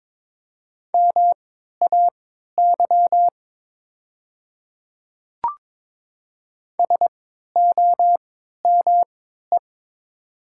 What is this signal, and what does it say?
Morse code, heard as a keyed tone, about 22 wpm, spaced as for 12 wpm, 700 Hz. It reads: MAY SOME
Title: 3x Character Spacing - Top 100 Words - 22wpm